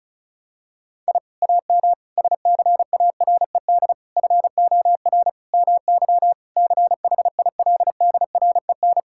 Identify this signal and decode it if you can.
Morse code, heard as a keyed tone, about 36 wpm, 700 Hz. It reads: I AM SCARED FOR MY CHILDREN
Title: Sentences from Top 1000 Words - Review Part 5 - 35wpm